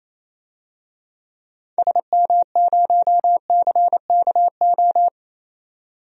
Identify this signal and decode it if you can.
Morse code, heard as a keyed tone, about 28 wpm, 700 Hz. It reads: SM0CKO
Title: DXCC-100 Call Signs - 6 Characters - 28wpm